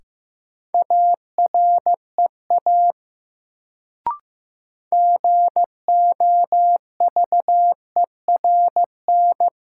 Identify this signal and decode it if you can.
Morse code, heard as a keyed tone, about 15 wpm, 700 Hz. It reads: AREA GOVERN
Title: Sentences from Top 1000 Words - Review Part 2 - New Word Drill - 15wpm